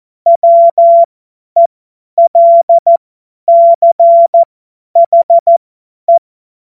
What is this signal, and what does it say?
Morse code, heard as a keyed tone, about 14 wpm, spaced as for 10 wpm, 680 Hz. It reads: WELCHE
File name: Morsekurs / Witze / Witz 02. Dezember 2025 50 F70